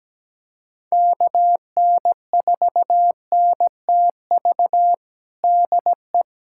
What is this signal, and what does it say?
Morse code, heard as a keyed tone, about 17 wpm, 700 Hz. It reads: KN4NTV DE
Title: USA Callsign DE Callsign 17wpm